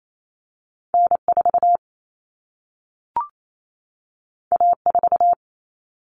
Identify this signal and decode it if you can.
Morse code, heard as a keyed tone, about 28 wpm, 700 Hz. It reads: N4 A4
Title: Common Cut Numbers - 2-Digit 28wpm